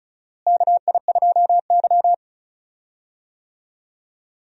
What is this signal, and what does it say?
Morse code, heard as a keyed tone, about 36 wpm, 700 Hz. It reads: KI2Y
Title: USA Call Signs - 4 Characters - Repeated 2x - 35wpm